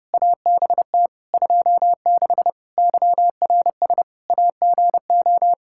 Morse code, heard as a keyed tone, about 30 wpm, 700 Hz. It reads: ABT 26 YRS AGO
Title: Practice QSO - Exchange 3 - More Info - 30wpm